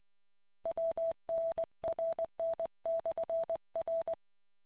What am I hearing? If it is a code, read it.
Morse code, heard as a keyed tone, about 20 wpm, 660 Hz. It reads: WNRN/R